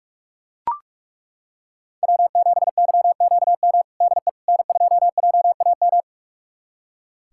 Morse code, heard as a keyed tone, about 42 wpm, 700 Hz. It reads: W7YQM DE N2JAM
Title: USA Callsign DE Callsign - 5 Characters - 45wpm